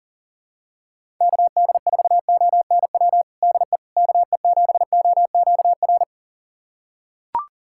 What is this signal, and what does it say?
Morse code, heard as a keyed tone, about 40 wpm, 700 Hz. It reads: KD4ONW DE KE7OQR
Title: USA Callsign DE Callsign - 6 Characters - 40wpm